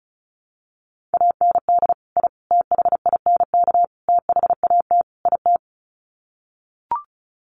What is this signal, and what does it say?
Morse code, heard as a keyed tone, about 34 wpm, 700 Hz. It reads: AND I THINK THAT IT
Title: Sets of 5 Words Encore2 35wpm